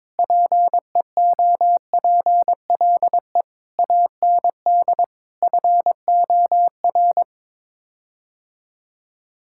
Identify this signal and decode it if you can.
Morse code, heard as a keyed tone, about 22 wpm, 700 Hz. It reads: PEOPLE AND FOR
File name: practice / 22wpm / Sets of 8 Words - 22wpm